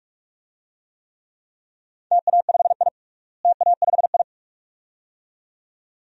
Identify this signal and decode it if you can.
Morse code, heard as a keyed tone, about 41 wpm, 700 Hz. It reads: TA5I TA5I
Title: DXCC-100 Call Signs - 4 Characters - Repeated 2x - 45wpm